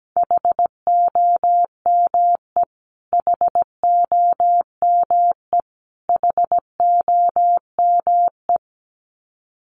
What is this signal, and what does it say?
Morse code, heard as a keyed tone, about 17 wpm, 700 Hz. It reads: HOME HOME HOME
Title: Top 200 Words Repeated 17wpm